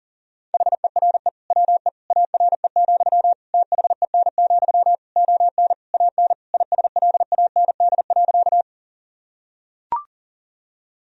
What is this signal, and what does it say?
Morse code, heard as a keyed tone, about 37 wpm, 700 Hz. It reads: HERE WE ARE, THEN, ON AN ISLAND.
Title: Sentences from Top 1000 Words - Review Part 3 - 40wpm